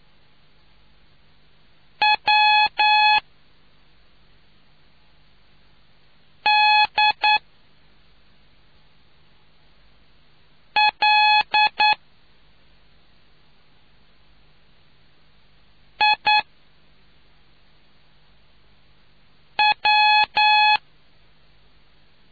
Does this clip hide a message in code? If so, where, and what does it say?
Yes: Morse code from 2.0 to 20.8 s, heard as a keyed tone, about 9 wpm, 850 Hz. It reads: W D L I W